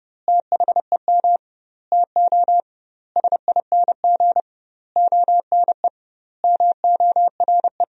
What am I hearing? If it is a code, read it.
Morse code, heard as a keyed tone, about 30 wpm, 700 Hz. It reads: THEM TO SING ONE MORE
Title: 2x Word-Spacing - Sentences from Top 400 Words - 30wpm